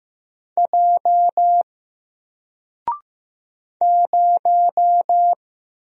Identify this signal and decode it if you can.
Morse code, heard as a keyed tone, about 15 wpm, 700 Hz. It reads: J 0
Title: Single Letter-Number Alt 15wpm